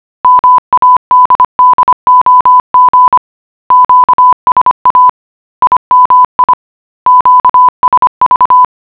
Morse code, heard as a keyed tone, about 25 wpm, 1000 Hz. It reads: MADDOG QSA IMI QSV